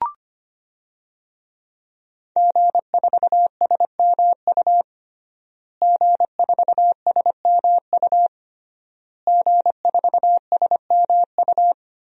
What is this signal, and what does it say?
Morse code, heard as a keyed tone, about 25 wpm, 700 Hz. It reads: G4SMU G4SMU G4SMU
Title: DXCC-100 Call Signs - 5 Characters - Repeated 3x - 25wpm